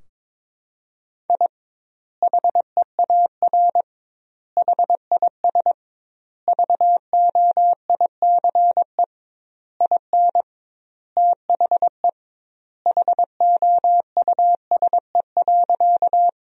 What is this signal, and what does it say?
Morse code, heard as a keyed tone, about 22 wpm, 700 Hz. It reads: I HEAR HIS VOICE IN THE HOUSE.
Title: 2x Word-Spacing - Sentences from Top 500 Words - 22wpm